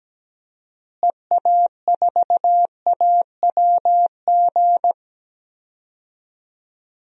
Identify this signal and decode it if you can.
Morse code, heard as a keyed tone, about 17 wpm, 700 Hz. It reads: EA4AWG
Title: DXCC-100 Call Signs - 6 Characters - Repeated 3x - 17wpm